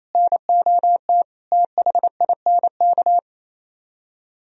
Morse code, heard as a keyed tone, about 34 wpm, 700 Hz. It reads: NOT THINK
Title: Speed-Racing - Sets of 4 Words 28wpm